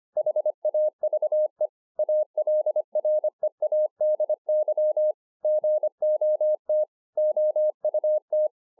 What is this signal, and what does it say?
Morse code, heard as a keyed tone, about 26 wpm, 610 Hz. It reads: HAVE ALREADY GOT OUT
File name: 2011-04-mp3 / THE 3.11 3 25WPM